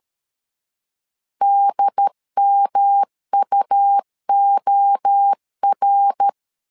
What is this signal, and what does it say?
Morse code, heard as a keyed tone, about 13 wpm, 800 Hz. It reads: DMUOR
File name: LEZIONI / 3 MWRO / 3 MWRO 12wpm 1730